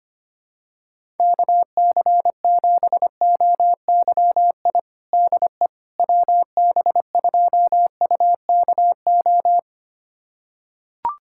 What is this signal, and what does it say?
Morse code, heard as a keyed tone, about 25 wpm, 700 Hz. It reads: KC7OYI DE WB2UKO
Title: USA Callsign DE Callsign - 6 Characters - 25wpm